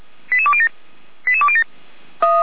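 Five-tone selective call (ZVEI), heard as quick runs of five tones, tones of about 70 ms. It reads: 88207 88207
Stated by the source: ZVEI five-tone fire brigade alerting call received by SDR, from Feuerwehr Orlamünde.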